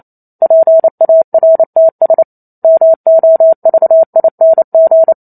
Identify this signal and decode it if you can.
Morse code, heard as a keyed tone, about 28 wpm, 650 Hz. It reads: PARTS MOVING
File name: Chapter 2 - 28 WPM